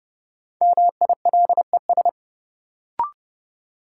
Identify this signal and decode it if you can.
Morse code, heard as a keyed tone, about 30 wpm, 700 Hz. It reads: MILES
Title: Sentences from Top 1000 Words - Review Part 3 - New Word Drill - 30wpm